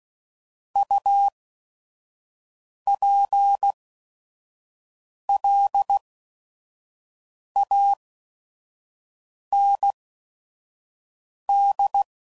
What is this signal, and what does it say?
Morse code, heard as a keyed tone, about 16 wpm, spaced as for 5 wpm, 800 Hz. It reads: UPLAND
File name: CW Code Groups @5WPM / fromSM's / Cities